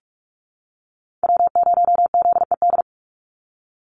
Morse code, heard as a keyed tone, about 44 wpm, 700 Hz. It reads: W0ZED
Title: USA Call Signs 45wpm